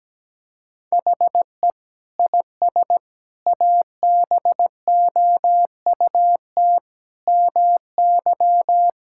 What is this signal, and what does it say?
Morse code, heard as a keyed tone, about 17 wpm, 700 Hz. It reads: HE IS ABOUT MY
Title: Sentences from Top 1600 Words 17wpm